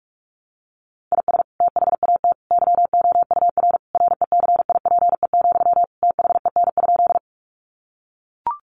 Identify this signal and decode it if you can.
Morse code, heard as a keyed tone, about 44 wpm, 700 Hz. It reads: IS THAT YOUR RECIPE, THEN?
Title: Sentences from Top 1200 Words 45wpm